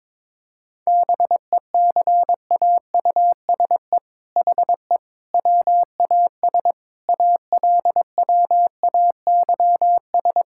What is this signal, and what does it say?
Morse code, heard as a keyed tone, about 22 wpm, 700 Hz. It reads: BECAUSE HE WAS ALWAYS
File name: All-Age Appropriate Jokes - 22wpm